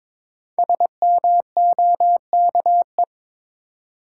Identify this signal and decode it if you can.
Morse code, heard as a keyed tone, about 22 wpm, 700 Hz. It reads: SMOKE
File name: 5 Letter Words - First 3 Letters Given - 22wpm